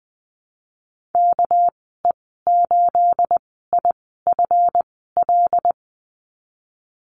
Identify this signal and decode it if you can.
Morse code, heard as a keyed tone, about 20 wpm, 700 Hz. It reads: K E 8 I F L
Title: 231 Lesson 37g - Letter x - Call Signs 20wpm